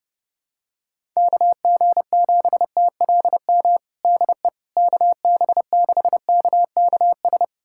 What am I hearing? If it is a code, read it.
Morse code, heard as a keyed tone, about 32 wpm, 700 Hz. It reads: KG7TLM DE KB6KKS
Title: USA Callsign DE Callsign - 6 Characters - 30wpm